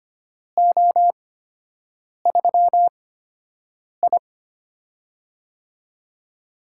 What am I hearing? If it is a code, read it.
Morse code, heard as a keyed tone, about 25 wpm, spaced as for 7 wpm, 700 Hz. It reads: O3I
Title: USA Call Signs - 4 Characters - 8x Character Spacing - 25wpm